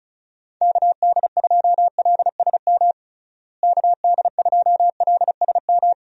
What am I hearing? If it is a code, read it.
Morse code, heard as a keyed tone, about 36 wpm, 700 Hz. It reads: KD2LSM KD2LSM
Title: USA Call Signs - 6 Characters - Repeated 3x - 35wpm